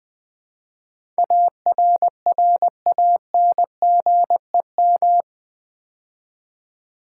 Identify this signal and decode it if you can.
Morse code, heard as a keyed tone, about 20 wpm, 700 Hz. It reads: ARRANGEM
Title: Build A Word - 4 Letters At A Time - 11-Letter Words - 20wpm